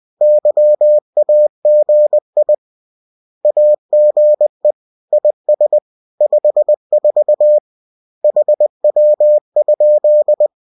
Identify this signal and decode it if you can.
Morse code, heard as a keyed tone, about 20 wpm, 600 Hz. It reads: YAGI AGE IS 54 HW?